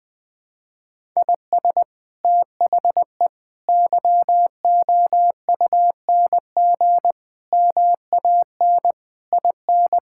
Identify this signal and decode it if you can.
Morse code, heard as a keyed tone, about 21 wpm, 700 Hz. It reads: IS THE YOUNG MAN IN
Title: Sentences from Top 1600 Words 20wpm